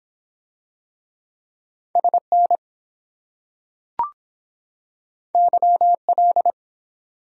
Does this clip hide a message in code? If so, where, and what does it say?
Yes: Morse code from 2.0 to 6.5 s, heard as a keyed tone, about 22 wpm, 700 Hz. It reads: SN YL